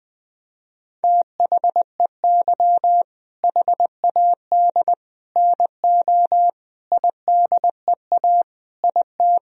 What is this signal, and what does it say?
Morse code, heard as a keyed tone, about 20 wpm, 700 Hz. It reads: THEY HAD NO IDEA IT
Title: Sentences from Top 1000 Words - Review Part 1 - 20wpm